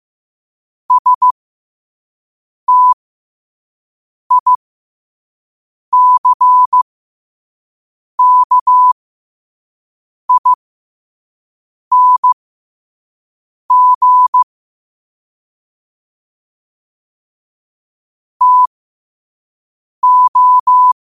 Morse code, heard as a keyed tone, about 15 wpm, spaced as for 5 wpm, 1000 Hz. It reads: STICKING TO
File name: qotdfiles / 5wpm